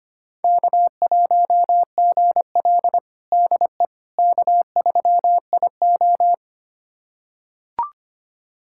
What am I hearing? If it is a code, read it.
Morse code, heard as a keyed tone, about 25 wpm, 700 Hz. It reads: K1GL DE K3IO
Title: USA Callsign DE Callsign - 4 Characters - 25wpm